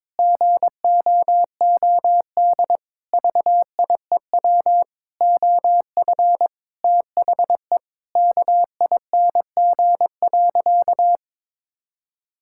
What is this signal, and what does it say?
Morse code, heard as a keyed tone, about 22 wpm, 700 Hz. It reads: GOOD VIEW OF THE KING.
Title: Sentences from Top 1000 Words - Review Part 6 - 22wpm